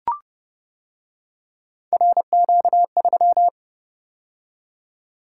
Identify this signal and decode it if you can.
Morse code, heard as a keyed tone, about 28 wpm, 700 Hz. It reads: RQ3